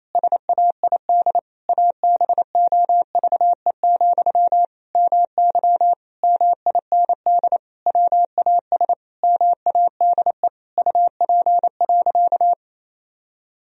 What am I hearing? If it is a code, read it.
Morse code, heard as a keyed tone, about 28 wpm, 700 Hz. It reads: SAID ABOVE, MY MIND WAS MADE UP.
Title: Sentences from Top 1000 Words - Review Part 2 - 28wpm